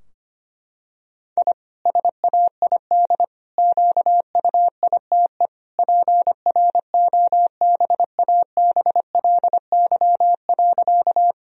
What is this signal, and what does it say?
Morse code, heard as a keyed tone, about 25 wpm, 700 Hz. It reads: I SAID QUITE PROBABLY.
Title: Sentences from Top 1000 Words - Review Part 4 - 25wpm